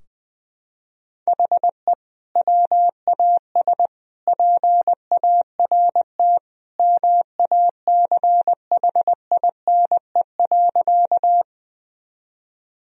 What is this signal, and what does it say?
Morse code, heard as a keyed tone, about 20 wpm, 700 Hz. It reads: HE WAS PART MACHINE.
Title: Sentences from Top 1000 Words - Review Part 2 - 20wpm